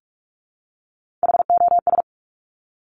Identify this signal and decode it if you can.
Morse code, heard as a keyed tone, about 46 wpm, 700 Hz. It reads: HOS
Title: Top Three Letter Combinations in Words 45wpm